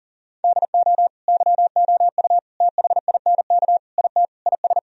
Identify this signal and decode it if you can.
Morse code, heard as a keyed tone, about 40 wpm, 700 Hz. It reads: DO YOU THINK IT IS
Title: Sentences from Top 1000 Words - Review Part 1 - 40wpm